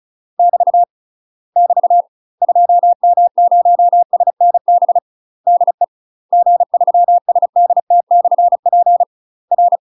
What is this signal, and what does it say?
Morse code, heard as a keyed tone, about 36 wpm, 700 Hz. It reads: <BT> <BT> 2M0SNB DE G3SDT/P R
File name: UK MASTER QSOs TRAINING / 35WPM / UK MASTER QSOs 0010